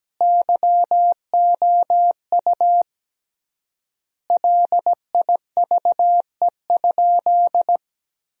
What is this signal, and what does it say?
Morse code, heard as a keyed tone, about 17 wpm, 700 Hz. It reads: YOU LIVE?
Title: 3x Word-Spacing - Sentences from Top 200 Words - 17wpm